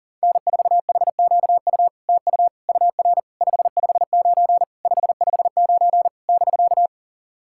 Morse code, heard as a keyed tone, about 40 wpm, 700 Hz. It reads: N4HQU TU UR 559 559 <BK>
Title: SOTA QSO Practice 40wpm